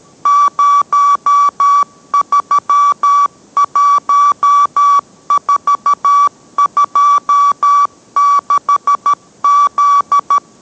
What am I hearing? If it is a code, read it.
Morse code, heard as a keyed tone, about 13 wpm, 1150 Hz. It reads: 031426Z